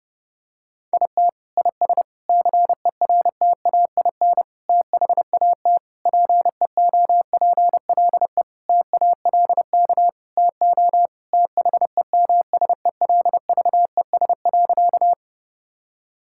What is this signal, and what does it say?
Morse code, heard as a keyed tone, about 30 wpm, 700 Hz. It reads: IT IS CERTAIN THAT PEOPLE TALK TO THEMSELVES.
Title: Sentences from Top 1000 Words - Review Part 5 - 30wpm